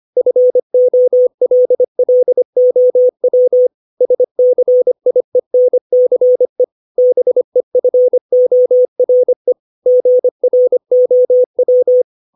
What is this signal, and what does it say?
Morse code, heard as a keyed tone, about 25 wpm, 500 Hz. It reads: FOLLOW SCIENCE BEFORE GROW
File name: 632 words / 632 Words 25WPM